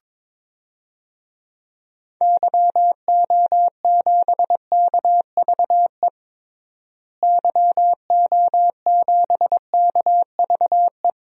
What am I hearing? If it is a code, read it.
Morse code, heard as a keyed tone, about 22 wpm, 700 Hz. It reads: YO7KVE YO7KVE
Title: DXCC-100 Call Signs - 6 Characters - Repeated 3x - 22wpm